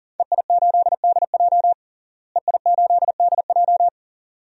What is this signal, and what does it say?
Morse code, heard as a keyed tone, about 40 wpm, 700 Hz. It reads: EI8DJ EI8DJ
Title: DXCC-100 Call Signs - 5 Characters - Repeated 2x - 40wpm